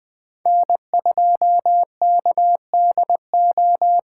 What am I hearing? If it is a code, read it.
Morse code, heard as a keyed tone, about 20 wpm, 700 Hz. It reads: N2KDO
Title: Speed Racing - USA 1x3 Call Signs 17wpm